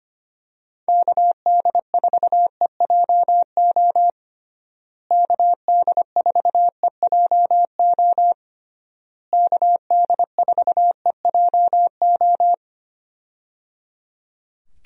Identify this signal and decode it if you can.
Morse code, heard as a keyed tone, about 25 wpm, 700 Hz. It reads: KD4EJO KD4EJO KD4EJO
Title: USA Call Signs - 6 Characters - Repeated 3x - 25wpm